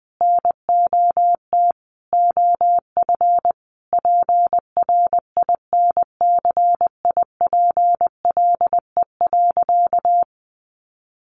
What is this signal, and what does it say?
Morse code, heard as a keyed tone, about 20 wpm, 700 Hz. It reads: NOT OF PRINCIPLE.
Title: Sentences from Top 1400 Words 20wpm